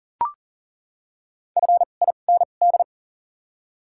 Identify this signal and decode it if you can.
Morse code, heard as a keyed tone, about 40 wpm, spaced as for 29 wpm, 700 Hz. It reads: FIND
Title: 2x Character Spacing - Top 100 Words - 40wpm